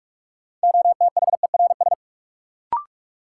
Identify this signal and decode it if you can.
Morse code, heard as a keyed tone, about 46 wpm, 700 Hz. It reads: OTHERS
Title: Sentences from Top 1500 Words - New Word Drill Alt - 45wpm